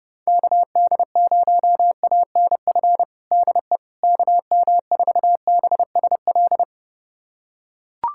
Morse code, heard as a keyed tone, about 30 wpm, 700 Hz. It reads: KD0ANF DE KM4BSL
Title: USA Callsign DE Callsign - 6 Characters - 30wpm